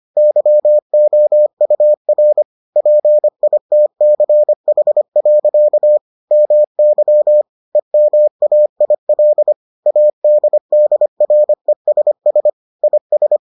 Morse code, heard as a keyed tone, about 25 wpm, 600 Hz. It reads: YOUR PITCH. MY EMAIL ADDRESS IS